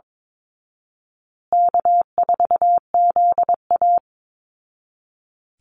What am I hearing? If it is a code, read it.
Morse code, heard as a keyed tone, about 22 wpm, 700 Hz. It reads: K4ZA
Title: Easy Call Signs Repeated 3x - 22wpm